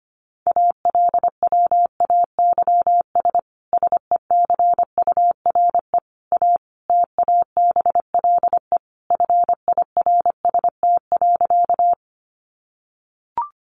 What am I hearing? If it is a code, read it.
Morse code, heard as a keyed tone, about 25 wpm, 700 Hz. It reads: ALWAYS SECURE A TABLE FIRST.
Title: Sentences from Top 1200 Words 25wpm